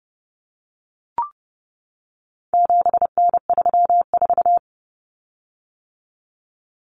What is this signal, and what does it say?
Morse code, heard as a keyed tone, about 30 wpm, 700 Hz. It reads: 7N34
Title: Common Cut Numbers - 4-Digit 30wpm